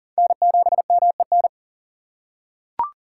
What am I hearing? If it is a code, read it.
Morse code, heard as a keyed tone, about 40 wpm, 700 Hz. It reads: N7MEN
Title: USA Call Signs - 5 Characters - 2x Character Spacing - 40wpm